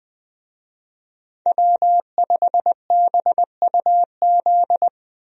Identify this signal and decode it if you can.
Morse code, heard as a keyed tone, about 20 wpm, 700 Hz. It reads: W5BUZ
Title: USA Call Signs - 5 Characters - Repeated 2x - 20wpm